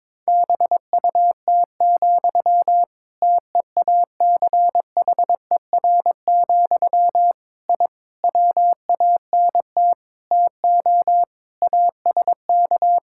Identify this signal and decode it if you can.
Morse code, heard as a keyed tone, about 22 wpm, 700 Hz. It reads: BUT, TEACHER, I WANT TO ASK
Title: Sentences from Top 1000 Words - Review Part 6 - 22wpm